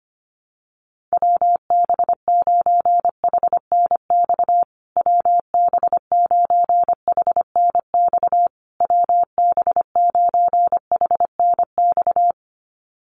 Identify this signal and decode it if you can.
Morse code, heard as a keyed tone, about 25 wpm, 700 Hz. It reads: WB9HNX WB9HNX WB9HNX
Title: USA Call Signs Repeated 3x 25wpm